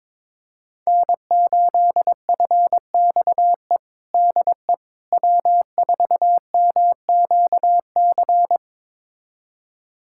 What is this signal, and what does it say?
Morse code, heard as a keyed tone, about 22 wpm, 700 Hz. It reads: N8FXE DE W4MQC
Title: USA Callsign DE Callsign 22wpm